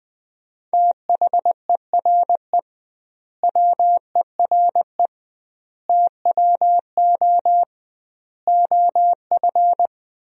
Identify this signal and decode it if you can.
Morse code, heard as a keyed tone, about 20 wpm, 700 Hz. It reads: THERE WERE TWO OF
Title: 2x Word-Spacing - Sentences from Top 100 Words - 20wpm